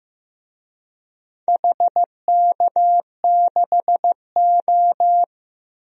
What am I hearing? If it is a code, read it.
Morse code, heard as a keyed tone, about 15 wpm, 700 Hz. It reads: HK6O
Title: DXCC-100 Call Signs - 4 Characters - Repeated 2x - 15wpm